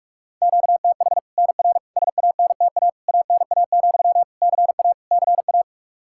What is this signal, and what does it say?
Morse code, heard as a keyed tone, about 46 wpm, 700 Hz. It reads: QTH NR SANTA ANA, CA CA
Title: Practice QSO - Exchange 1 - Beginning - 45wpm